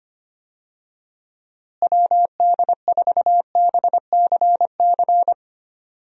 Morse code, heard as a keyed tone, about 25 wpm, 700 Hz. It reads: WD4BCC
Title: USA Call Signs - 6 Characters - Repeated 3x - 25wpm